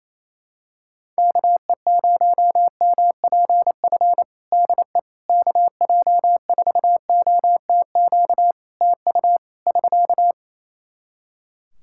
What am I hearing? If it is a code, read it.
Morse code, heard as a keyed tone, about 28 wpm, 700 Hz. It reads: KE0MPF DE KJ4OTQ TU <SK>
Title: Practice QSO - Exchange 4 - Ending - 28wpm